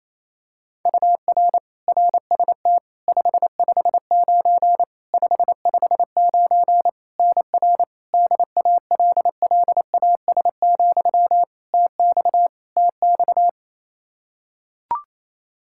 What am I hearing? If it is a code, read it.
Morse code, heard as a keyed tone, about 28 wpm, 700 Hz. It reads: UR RST 559 559 NR DALLAS, TX TX
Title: Practice QSO - Exchange 1 - Beginning - 28wpm